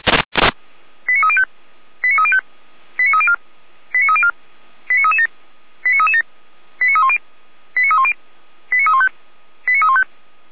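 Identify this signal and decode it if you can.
Five-tone selective call (ZVEI), heard as quick runs of five tones, tones of about 70 ms. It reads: 89305 89305 89304 89304 89337 89337 89310 89310 89315 89315
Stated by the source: ZVEI five-tone selective call 89276 received by SDR, from Feuerwehr Orlamünde audio server.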